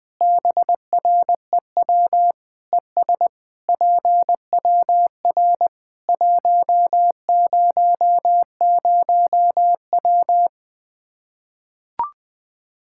Morse code, heard as a keyed tone, about 20 wpm, 700 Hz. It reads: BREW ES PWR 100W
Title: Practice QSO - Exchange 2 - Info - 20wpm